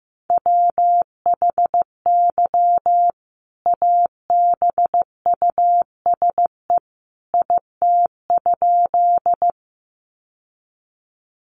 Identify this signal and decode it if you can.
Morse code, heard as a keyed tone, about 15 wpm, 700 Hz. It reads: WHY ABUSE IT?
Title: Sentences from Top 1300 Words 15wpm